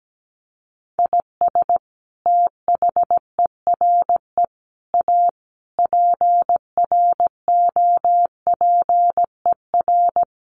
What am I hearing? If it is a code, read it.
Morse code, heard as a keyed tone, about 17 wpm, 700 Hz. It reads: IS THERE A PROPER
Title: Sentences from Top 800 Words 17wpm